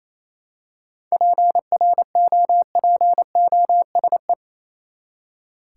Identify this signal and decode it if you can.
Morse code, heard as a keyed tone, about 28 wpm, 700 Hz. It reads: PROPOSE
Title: Seven Letter Words - 28wpm